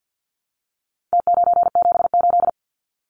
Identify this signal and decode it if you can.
Morse code, heard as a keyed tone, about 50 wpm, 700 Hz. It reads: T978
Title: Numbers - Contest Style - 4-Digit 50wpm